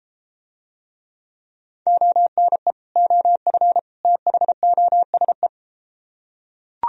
Morse code, heard as a keyed tone, about 31 wpm, 700 Hz. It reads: ONE OF THOSE